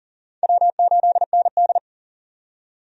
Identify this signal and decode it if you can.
Morse code, heard as a keyed tone, about 40 wpm, 700 Hz. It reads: W8ND
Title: USA Call Signs - 4 Characters - 40wpm